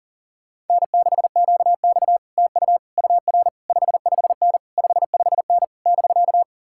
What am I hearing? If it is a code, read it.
Morse code, heard as a keyed tone, about 50 wpm, 700 Hz. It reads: N6QX TU UR 55N 55N <BK>